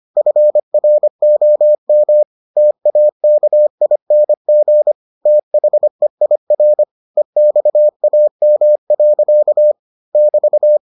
Morse code, heard as a keyed tone, about 25 wpm, 600 Hz. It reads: FROM TAKING THEIR EXAM. <BT>